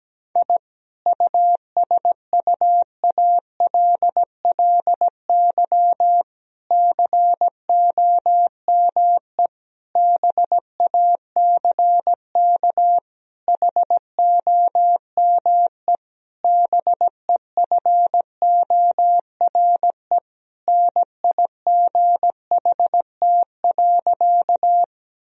Morse code, heard as a keyed tone, about 17 wpm, 700 Hz. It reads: I USUALLY COME BACK HOME BEFORE NIGHT.
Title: Sentences from Top 1000 Words - Review Part 3 - 17wpm